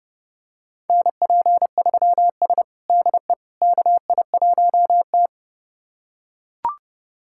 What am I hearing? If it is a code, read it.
Morse code, heard as a keyed tone, about 30 wpm, 700 Hz. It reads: NP3S DE KI1T
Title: USA Callsign DE Callsign - 4 Characters - 30wpm